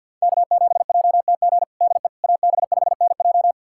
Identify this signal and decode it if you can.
Morse code, heard as a keyed tone, about 47 wpm, 700 Hz. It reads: K7JTG DE AB5NJ
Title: USA Callsign DE Callsign - 5 Characters - 50wpm